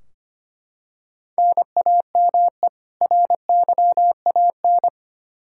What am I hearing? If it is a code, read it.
Morse code, heard as a keyed tone, about 25 wpm, 700 Hz. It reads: NAME RYAN
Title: Practice QSO - Exchange 1 - Beginning - 25wpm